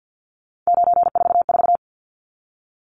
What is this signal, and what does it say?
Morse code, heard as a keyed tone, about 50 wpm, 700 Hz. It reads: 944
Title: Numbers - 3-Digit 50wpm